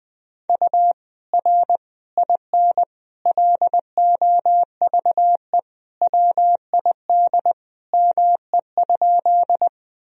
Morse code, heard as a keyed tone, about 20 wpm, 700 Hz. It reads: U R IN LOVE WID ME?
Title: Sentences from Top 400 Words and Abbreviations 20wpm